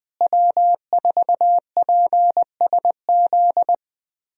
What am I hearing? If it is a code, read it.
Morse code, heard as a keyed tone, about 20 wpm, 700 Hz. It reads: W4PSZ
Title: USA Call Signs - 5 Characters - Repeated 3x - 20wpm